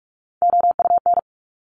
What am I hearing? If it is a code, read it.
Morse code, heard as a keyed tone, about 46 wpm, 700 Hz. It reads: OUN
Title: Top Three Letter Combinations in Words 45wpm